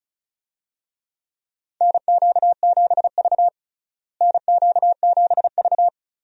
Morse code, heard as a keyed tone, about 34 wpm, 700 Hz. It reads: NQ7V NQ7V
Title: USA Call Signs - 4 Characters - Repeated 3x - 35wpm